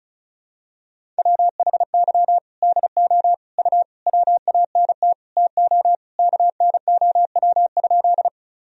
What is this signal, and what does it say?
Morse code, heard as a keyed tone, about 34 wpm, 700 Hz. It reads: WHY DO U WANT TO KNOW?
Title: Sentences from Top 200 Words and Abbreviations 35wpm